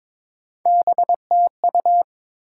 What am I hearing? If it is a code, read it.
Morse code, heard as a keyed tone, about 22 wpm, 700 Hz. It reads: BTU
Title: Top 180 QSO Elements 22wpm